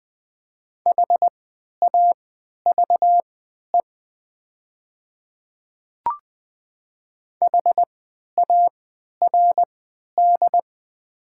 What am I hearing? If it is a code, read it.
Morse code, heard as a keyed tone, about 20 wpm, spaced as for 11 wpm, 700 Hz. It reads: HAVE HARD
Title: 3x Character Spacing - Top 500 Words - 20wpm